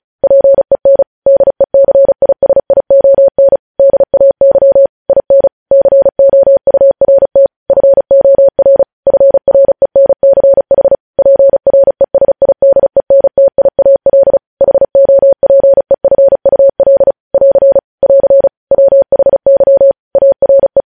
Morse code, heard as a keyed tone, about 36 wpm, 550 Hz. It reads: PEN DECISION DAY IN COURT FOR FRENCH PRESIDENTIAL HOPEFUL <AR> <AR> WHY ARE
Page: News headlines in Morse code at 35 wpm